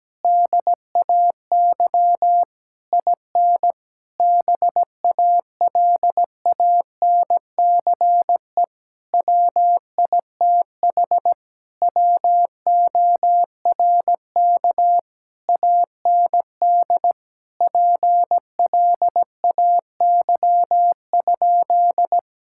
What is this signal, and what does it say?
Morse code, heard as a keyed tone, about 17 wpm, 700 Hz. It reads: DAY IN BALANCE WITH WORK AND PLAY?
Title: Sentences from Top 1000 Words - Review Part 6 - 17wpm